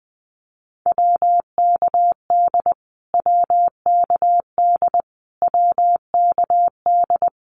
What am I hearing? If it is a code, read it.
Morse code, heard as a keyed tone, about 20 wpm, 700 Hz. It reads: WKD WKD WKD
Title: Abbreviations Repeated 3x - 20wpm